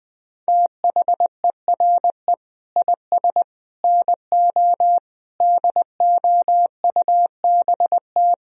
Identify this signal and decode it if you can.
Morse code, heard as a keyed tone, about 22 wpm, 700 Hz. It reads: THERE IS NO DOUBT
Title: Speed-Racing - Sets of 5 Words 20wpm